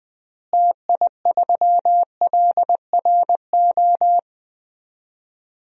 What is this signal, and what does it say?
Morse code, heard as a keyed tone, about 20 wpm, 690 Hz. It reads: TI3LRO
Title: DXCC-100 Call Signs - 6 Characters - Repeated 2x - 20wpm